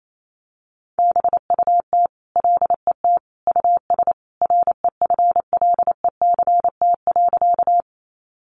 Morse code, heard as a keyed tone, about 28 wpm, 700 Hz. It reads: BUT LET US REFLECT.
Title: Sentences from Top 1500 Words 28wpm